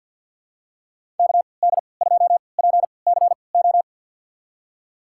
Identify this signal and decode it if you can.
Morse code, heard as a keyed tone, about 53 wpm, spaced as for 29 wpm, 700 Hz. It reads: KD2PCO
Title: USA Call Signs - 6 Characters - 3x Character Spacing - 50wpm